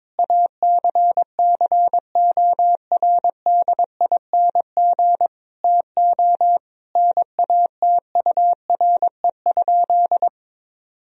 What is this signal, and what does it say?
Morse code, heard as a keyed tone, about 22 wpm, 700 Hz. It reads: ACCORDING TO NATURE?
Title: Sentences from Top 1000 Words - Review Part 4 - 22wpm